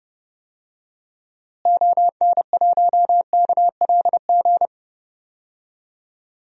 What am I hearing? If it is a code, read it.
Morse code, heard as a keyed tone, about 30 wpm, 700 Hz. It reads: ON1KLG
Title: DXCC-100 Call Signs - 6 Characters - Repeated 3x - 30wpm